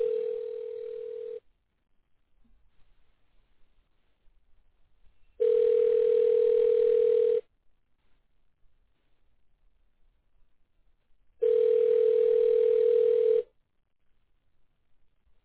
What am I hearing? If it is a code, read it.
Ringback tone (North America), 440 + 480 Hz.